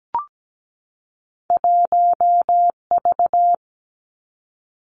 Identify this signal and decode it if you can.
Morse code, heard as a keyed tone, about 17 wpm, 700 Hz. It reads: 1V
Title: Two Challenging Letter-Number Combinations 17wpm